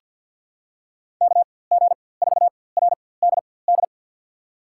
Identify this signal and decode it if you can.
Morse code, heard as a keyed tone, about 51 wpm, spaced as for 23 wpm, 700 Hz. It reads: KG4RDD
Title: USA Call Signs - 6 Characters - 4x Character Spacing - 50wpm